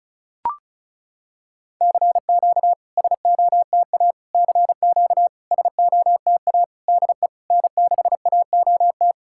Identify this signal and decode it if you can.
Morse code, heard as a keyed tone, about 34 wpm, 690 Hz. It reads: CQ SOTA CQ SOTA DE N6AOT
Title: SOTA QSO Practice 35wpm